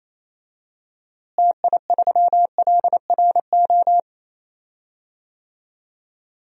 Morse code, heard as a keyed tone, about 27 wpm, 700 Hz. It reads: TI3LRO
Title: DXCC-100 Call Signs - 6 Characters - Repeated 2x - 28wpm